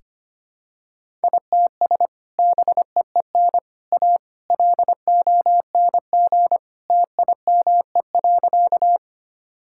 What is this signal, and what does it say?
Morse code, heard as a keyed tone, about 25 wpm, 700 Hz. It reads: ITS BEEN A LONG TIME.
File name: practice / 25wpm / Sentences with 3-Word Phrases - 25wpm